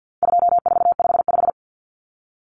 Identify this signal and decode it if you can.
Morse code, heard as a keyed tone, about 50 wpm, 700 Hz. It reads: 2455
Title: Common Cut Numbers - 4-Digit 50wpm